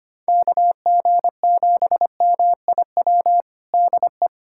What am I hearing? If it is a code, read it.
Morse code, heard as a keyed tone, about 25 wpm, 700 Hz. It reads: KG7MIW DE
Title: USA Callsign DE Callsign 25wpm